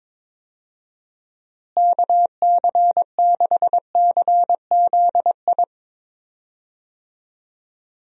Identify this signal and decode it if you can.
Morse code, heard as a keyed tone, about 22 wpm, 700 Hz. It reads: KC6CZI
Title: USA Call Signs - 6 Characters - Repeated 2x - 22wpm